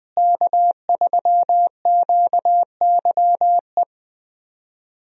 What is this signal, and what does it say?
Morse code, heard as a keyed tone, about 21 wpm, 700 Hz. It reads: K3QYE
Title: USA Call Signs - 5 Characters - Repeated 3x - 20wpm